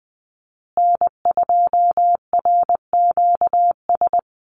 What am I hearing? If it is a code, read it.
Morse code, heard as a keyed tone, about 20 wpm, 700 Hz. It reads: N2RQS
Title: USA Call Signs Repeated 2x 20wpm